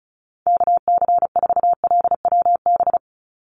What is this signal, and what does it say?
Morse code, heard as a keyed tone, about 34 wpm, 700 Hz. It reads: KC4LWB
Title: USA Call Signs 35wpm